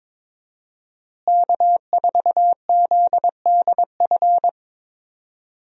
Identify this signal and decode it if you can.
Morse code, heard as a keyed tone, about 22 wpm, 700 Hz. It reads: K4ZDF
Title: USA Call Signs - 5 Characters - Repeated 3x - 22wpm